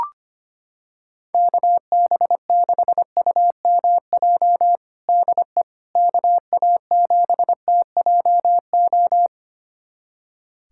Morse code, heard as a keyed tone, about 25 wpm, 700 Hz. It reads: KB6UMJ DE KA7TJO
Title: USA Callsign DE Callsign - 6 Characters - 25wpm